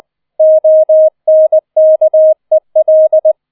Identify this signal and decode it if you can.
Morse code, heard as a keyed tone, about 20 wpm, 630 Hz. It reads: ONKEL